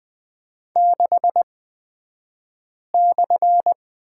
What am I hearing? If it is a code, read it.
Morse code, heard as a keyed tone, about 20 wpm, spaced as for 5 wpm, 700 Hz. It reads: 6/
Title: 235 Lesson 38d - Letter q - Two Characters 20wpm